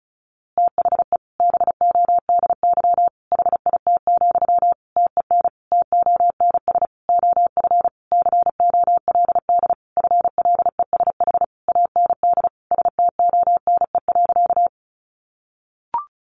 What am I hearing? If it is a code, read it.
Morse code, heard as a keyed tone, about 34 wpm, 700 Hz. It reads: THE BODY HIT, TEN TONS OF COLD FLESH AND STONE.
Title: A Sound Of Thunder 35wpm